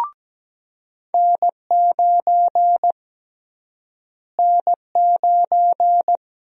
Morse code, heard as a keyed tone, about 17 wpm, 700 Hz. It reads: N9 N9
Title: USA Call Sign Prefixes - Repeated 2x - 17wpm